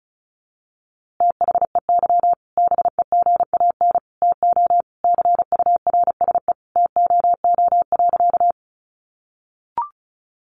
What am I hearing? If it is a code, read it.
Morse code, heard as a keyed tone, about 34 wpm, 700 Hz. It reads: THEY BEGAN TO CURSE TOO.
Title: A Sound Of Thunder 35wpm